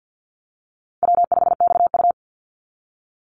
Morse code, heard as a keyed tone, about 50 wpm, 700 Hz. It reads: W5XU
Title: Easy Call Signs Repeated 3x - 50wpm